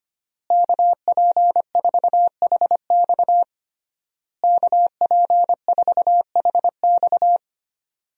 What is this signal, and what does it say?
Morse code, heard as a keyed tone, about 25 wpm, 700 Hz. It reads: KP4HX KP4HX
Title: USA Call Signs - 5 Characters - Repeated 3x - 25wpm